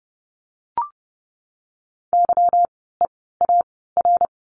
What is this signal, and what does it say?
Morse code, heard as a keyed tone, about 30 wpm, spaced as for 17 wpm, 700 Hz. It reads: YEAR
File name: course / 30:10wpm / 230 Lesson 37f - Letter x - Words 30wpm